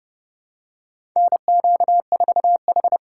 Morse code, heard as a keyed tone, about 30 wpm, 700 Hz. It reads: NQ4H
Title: USA Call Signs - 4 Characters - 8x Character Spacing - 30wpm